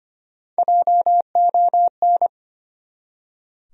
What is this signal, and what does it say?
Morse code, heard as a keyed tone, about 25 wpm, 700 Hz. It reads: JON
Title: USA Common Names - Max 3 Letters - 25wpm